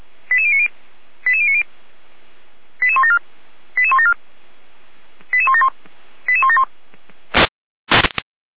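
Five-tone selective call (ZVEI), heard as quick runs of five tones, tones of about 70 ms. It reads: 88090 88090 88174 88174 88171 88171
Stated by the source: ZVEI five-tone call received on an SDR.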